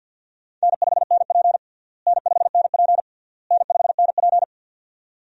Morse code, heard as a keyed tone, about 52 wpm, 700 Hz. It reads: N5NP N5NP N5NP
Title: USA Call Signs - 4 Characters - Repeated 3x - 50wpm